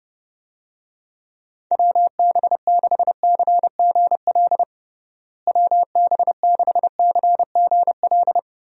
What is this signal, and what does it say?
Morse code, heard as a keyed tone, about 30 wpm, 700 Hz. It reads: WB6CGL WB6CGL
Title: USA Call Signs - 6 Characters - Repeated 2x - 30wpm